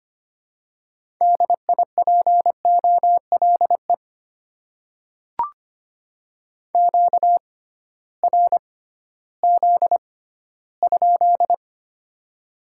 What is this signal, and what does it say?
Morse code, heard as a keyed tone, about 25 wpm, spaced as for 15 wpm, 700 Hz. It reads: DIPOLE QRZ?
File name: Top 90 QSO Elements - 6x Character Spacing - 25wpm